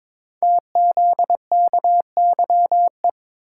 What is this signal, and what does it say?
Morse code, heard as a keyed tone, about 22 wpm, 700 Hz. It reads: TZKYE
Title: Code Groups - 5 Letters - 22wpm